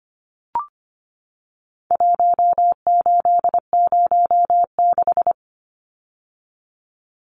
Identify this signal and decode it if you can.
Morse code, heard as a keyed tone, about 25 wpm, 700 Hz. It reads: 1806
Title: Numbers - 4-Digit 25wpm